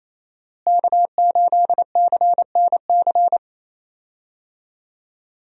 Morse code, heard as a keyed tone, about 27 wpm, 700 Hz. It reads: K8CNC
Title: USA Call Signs - 5 Characters - Repeated 3x - 28wpm